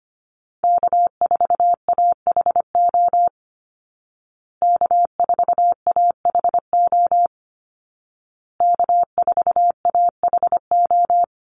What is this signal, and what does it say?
Morse code, heard as a keyed tone, about 25 wpm, 700 Hz. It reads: K4AHO K4AHO K4AHO
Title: Call Signs Repeated 2x - 25wpm